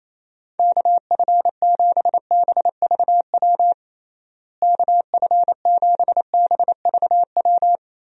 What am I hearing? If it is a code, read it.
Morse code, heard as a keyed tone, about 28 wpm, 700 Hz. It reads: KF7BVW KF7BVW
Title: USA Call Signs - 6 Characters - Repeated 3x - 28wpm